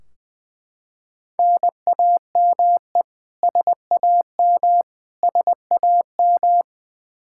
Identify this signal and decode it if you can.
Morse code, heard as a keyed tone, about 20 wpm, 700 Hz. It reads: NAME SAM SAM
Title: Practice QSO - Exchange 1 - Beginning - 20wpm